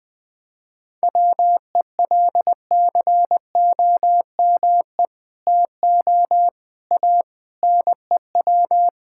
Morse code, heard as a keyed tone, about 20 wpm, 700 Hz. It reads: WELCOME TO A NEW
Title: Sentences with 2-Word Phrases - Spoken Beginning - 20wpm